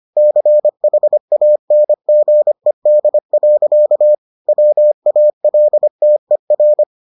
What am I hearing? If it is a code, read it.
Morse code, heard as a keyed tone, about 25 wpm, 600 Hz. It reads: CHANGED. WALTER